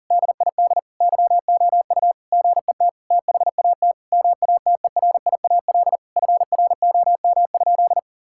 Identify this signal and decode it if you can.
Morse code, heard as a keyed tone, about 40 wpm, 700 Hz. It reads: DID YOU GET THAT MATERIAL FROM?
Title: Sentences from Top 1000 Words - Review Part 5 - 40wpm